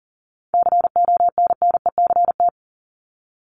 Morse code, heard as a keyed tone, about 40 wpm, 700 Hz. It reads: CONNECT
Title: Top 1251-1500 Words 40wpm